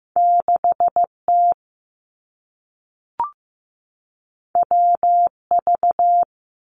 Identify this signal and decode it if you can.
Morse code, heard as a keyed tone, about 15 wpm, 700 Hz. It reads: 6 T W V
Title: Two Challenging Letter-Number Combinations 15wpm